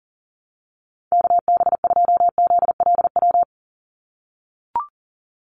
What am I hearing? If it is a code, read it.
Morse code, heard as a keyed tone, about 40 wpm, 700 Hz. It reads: KB2ZLW
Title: USA Call Signs 40wpm